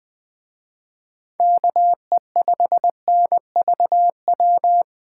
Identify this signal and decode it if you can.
Morse code, heard as a keyed tone, about 24 wpm, 700 Hz. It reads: KE5NVW
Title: Speed Racing - USA 2x3 Call Signs 20wpm